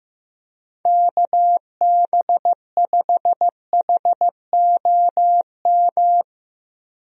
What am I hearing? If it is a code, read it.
Morse code, heard as a keyed tone, about 16 wpm, 700 Hz. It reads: KB5HOM
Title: USA Call Signs - 6 Characters - Repeated 3x - 15wpm